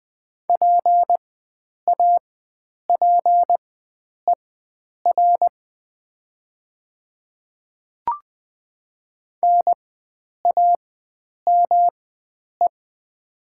Morse code, heard as a keyed tone, about 20 wpm, spaced as for 9 wpm, 700 Hz. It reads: PAPER NAME
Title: 4x Character Spacing - Top 300 Words - 20wpm